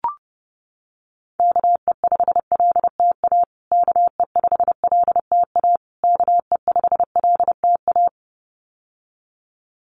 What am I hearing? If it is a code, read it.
Morse code, heard as a keyed tone, about 30 wpm, 700 Hz. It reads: KE5LTA KE5LTA KE5LTA
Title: USA Call Signs Repeated 3x 30wpm